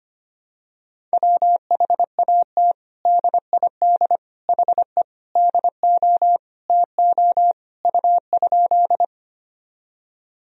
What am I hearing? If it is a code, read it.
Morse code, heard as a keyed tone, about 25 wpm, 700 Hz. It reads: WHAT DID HE DO TO U?
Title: Sentences from Top 100 Words and Abbreviations 25wpm